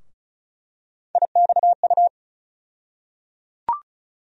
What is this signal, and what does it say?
Morse code, heard as a keyed tone, about 34 wpm, 700 Hz. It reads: IXU
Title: USA Call Sign Postfixes - 3 Letters 35wpm